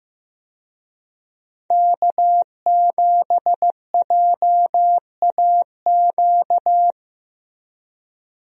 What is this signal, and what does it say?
Morse code, heard as a keyed tone, about 15 wpm, 700 Hz. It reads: K7JAQ
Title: USA Call Signs - 5 Characters - Repeated 2x - 15wpm